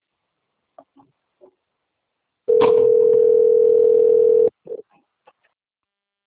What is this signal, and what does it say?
Ringback tone (North America), 440 + 480 Hz.